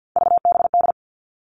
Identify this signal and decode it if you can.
Morse code, heard as a keyed tone, about 50 wpm, 700 Hz. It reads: VBD
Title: All Cut Numbers - 3-Digit 50wpm